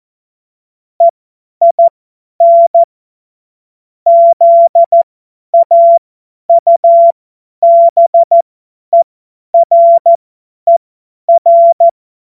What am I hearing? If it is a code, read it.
Morse code, heard as a keyed tone, about 14 wpm, spaced as for 10 wpm, 680 Hz. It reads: EIN ZAUBERER
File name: Morsekurs / Witze / Witz 04. Oktober 2022 50 F70